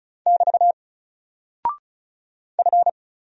Morse code, heard as a keyed tone, about 36 wpm, 700 Hz. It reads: <BT> F
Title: Single Character (Letter:Number:Punctuation:Prosign) 35wpm